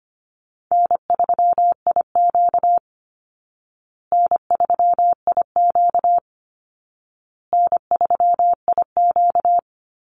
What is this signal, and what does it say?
Morse code, heard as a keyed tone, about 25 wpm, 700 Hz. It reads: N3IQ N3IQ N3IQ
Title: Easy Call Signs Repeated 3x - 25wpm